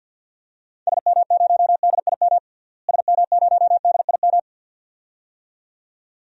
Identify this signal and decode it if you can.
Morse code, heard as a keyed tone, about 53 wpm, 700 Hz. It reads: SM0DIM SM0DIM
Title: DXCC-100 Call Signs - 6 Characters - Repeated 2x - 50wpm